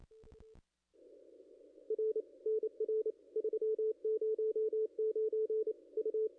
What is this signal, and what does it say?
Morse code, heard as a keyed tone, about 28 wpm, 440 Hz, with radio noise. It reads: R NR 309 U
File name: contests / ss-cw-k5tr-op-n5rz -0200